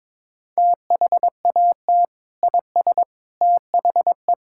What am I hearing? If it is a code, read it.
Morse code, heard as a keyed tone, about 22 wpm, 700 Hz. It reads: THAT IS THE
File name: practice / 22wpm / Sentences with 2-Word Phrases - 22wpm